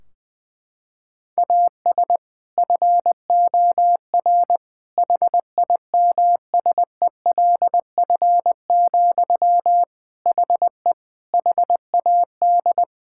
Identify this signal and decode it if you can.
Morse code, heard as a keyed tone, about 21 wpm, 700 Hz. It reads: AS FOR HIMSELF, HE HAD
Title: Sentences from Top 1600 Words 20wpm